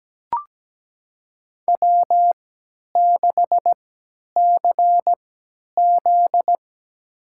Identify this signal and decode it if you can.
Morse code, heard as a keyed tone, about 17 wpm, spaced as for 10 wpm, 700 Hz. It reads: W6CZ
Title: USA Call Signs - 4 Characters - 3x Character Spacing - 17wpm